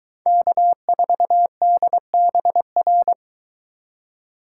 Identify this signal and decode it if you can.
Morse code, heard as a keyed tone, about 22 wpm, 700 Hz. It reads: K4DBR